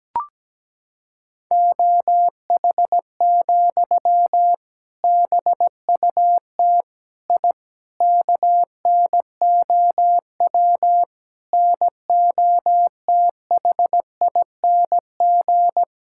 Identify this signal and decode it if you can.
Morse code, heard as a keyed tone, about 17 wpm, 700 Hz. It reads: OH, BUT I KNOW NOTHING
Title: Sentences from Top 1000 Words - Review Part 6 - 17wpm